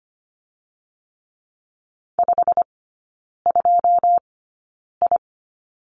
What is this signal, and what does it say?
Morse code, heard as a keyed tone, about 25 wpm, spaced as for 9 wpm, 700 Hz. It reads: 52I
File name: Lesson 15e - Number 5 - Three Characters 25wpm